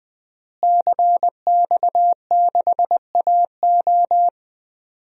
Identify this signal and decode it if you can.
Morse code, heard as a keyed tone, about 20 wpm, 700 Hz. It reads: CX6AO
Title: DXCC-100 Call Signs - 5 Characters - Repeated 3x - 20wpm